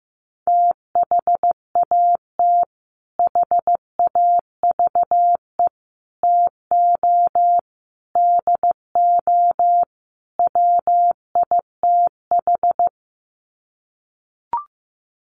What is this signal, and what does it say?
Morse code, heard as a keyed tone, about 15 wpm, 700 Hz. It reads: THAT HAVE TO DO WITH
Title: Sets of 5 Words Encore2 15wpm